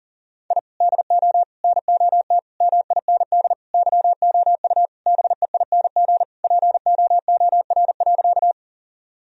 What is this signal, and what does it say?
Morse code, heard as a keyed tone, about 40 wpm, 700 Hz. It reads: I DO NOT MIND YOU BEING POOR.
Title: Sentences from Top 1000 Words - Review Part 3 - 40wpm